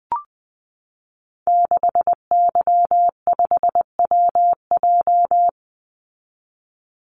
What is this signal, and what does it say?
Morse code, heard as a keyed tone, about 20 wpm, 700 Hz. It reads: 6Y5WJ
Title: Call Signs Repeated 3x - 20wpm